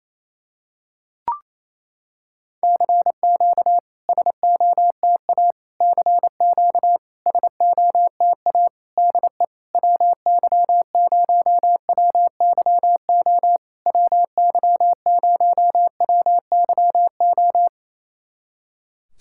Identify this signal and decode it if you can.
Morse code, heard as a keyed tone, about 28 wpm, 690 Hz. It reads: CQ SOTA CQ SOTA DE WY0WYO WY0WYO
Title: SOTA QSO Practice 28wpm